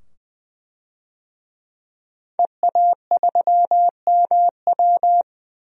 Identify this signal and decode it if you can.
Morse code, heard as a keyed tone, about 20 wpm, 700 Hz. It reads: EA3MW
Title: DXCC-100 Call Signs - 5 Characters - 20wpm